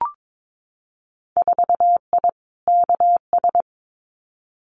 Morse code, heard as a keyed tone, about 22 wpm, 700 Hz. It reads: 4I KS
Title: Winter Field Day Exchanges Encore 22wpm